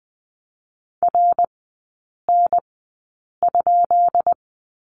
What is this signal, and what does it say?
Morse code, heard as a keyed tone, about 20 wpm, spaced as for 8 wpm, 690 Hz. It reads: RN?
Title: 172 Lesson 28e - Number 9 - Three Characters 20wpm